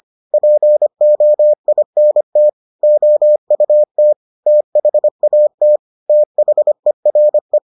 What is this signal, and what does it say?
Morse code, heard as a keyed tone, about 25 wpm, 600 Hz. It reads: POINT OUT THAT THERE